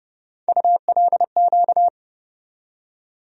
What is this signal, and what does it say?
Morse code, heard as a keyed tone, about 30 wpm, 700 Hz. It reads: ULQ
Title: USA Call Sign Postfixes - 3 Letters 30wpm